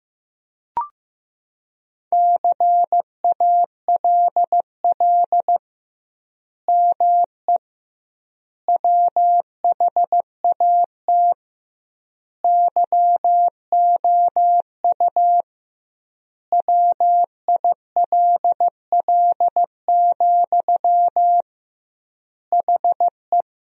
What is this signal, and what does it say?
Morse code, heard as a keyed tone, about 15 wpm, 700 Hz. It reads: CALL ME WHAT YOU WILL, HE
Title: 2x Word-Spacing - Sentences from Top 200 Words - 15wpm